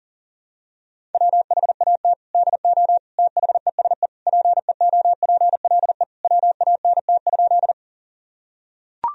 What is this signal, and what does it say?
Morse code, heard as a keyed tone, about 40 wpm, 700 Hz. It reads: WHAT DO THESE PEOPLE WANT?
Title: Sentences from Top 1000 Words - Review Part 1 - 40wpm